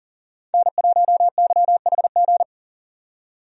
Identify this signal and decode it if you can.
Morse code, heard as a keyed tone, about 40 wpm, 700 Hz. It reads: N1YHG
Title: USA Call Signs - 5 Characters - 40wpm